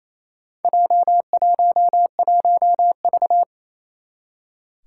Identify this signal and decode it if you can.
Morse code, heard as a keyed tone, about 27 wpm, 700 Hz. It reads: J11V
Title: Four Challenging Letter-Number Combinations 28wpm